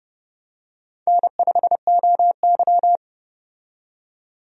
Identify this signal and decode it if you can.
Morse code, heard as a keyed tone, about 30 wpm, 700 Hz. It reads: N5OY
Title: USA Call Signs - 4 Characters - 30wpm